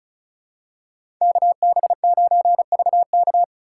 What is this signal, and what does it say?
Morse code, heard as a keyed tone, about 36 wpm, 700 Hz. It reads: KB9VK
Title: USA Call Signs - 4x Character Spacing - 35wpm